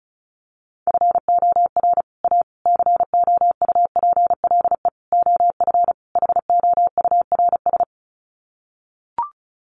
Morse code, heard as a keyed tone, about 34 wpm, 700 Hz. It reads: FOR A COUPLE OF HOURS
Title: Sets of 5 Words Encore2 35wpm